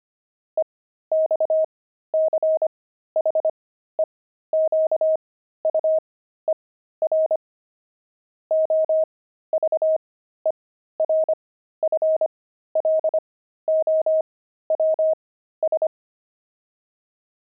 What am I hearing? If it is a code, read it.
Morse code, heard as a keyed tone, about 25 wpm, spaced as for 13 wpm, 630 Hz. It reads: EXCHEQUER OVERFLOWS
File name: pangram-fs-13-cs-25-kibitzer